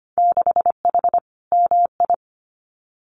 Morse code, heard as a keyed tone, about 25 wpm, 700 Hz. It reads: 6H MI
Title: Winter Field Day Exchanges Encore 25wpm